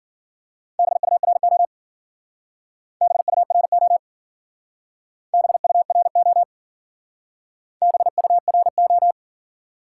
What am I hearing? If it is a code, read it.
Morse code, heard as a keyed tone, about 50 wpm, 700 Hz. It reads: BURO BURO BURO BURO